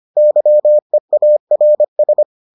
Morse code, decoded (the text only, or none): YEARS